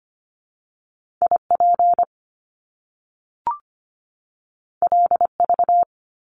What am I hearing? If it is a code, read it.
Morse code, heard as a keyed tone, about 25 wpm, 700 Hz. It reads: IP LV
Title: Top Two Letter Combinations in Words 25wpm